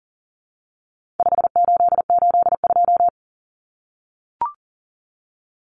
Morse code, heard as a keyed tone, about 38 wpm, 700 Hz. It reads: 5882
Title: Numbers - 4-Digit 40wpm